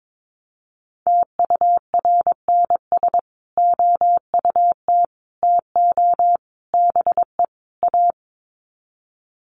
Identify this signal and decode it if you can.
Morse code, heard as a keyed tone, about 22 wpm, 700 Hz. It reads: TURNS OUT TO BE A
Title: Sets of 5 Words Encore 22wpm